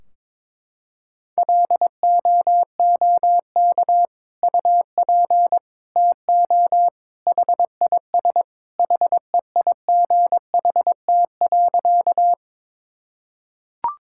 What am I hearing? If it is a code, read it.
Morse code, heard as a keyed tone, about 22 wpm, 700 Hz. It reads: LOOK UP TO HIS HEIGHT.
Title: Sentences from Top 1600 Words 22wpm